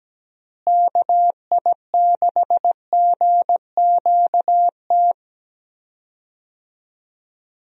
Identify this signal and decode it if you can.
Morse code, heard as a keyed tone, about 17 wpm, 700 Hz. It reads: KI6GQT
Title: USA Call Signs - 6 Characters - Repeated 2x - 17wpm